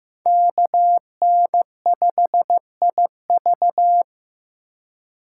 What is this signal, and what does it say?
Morse code, heard as a keyed tone, about 15 wpm, 700 Hz. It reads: KN5IV
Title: Code Groups - 5 Letter:Numbers - 15wpm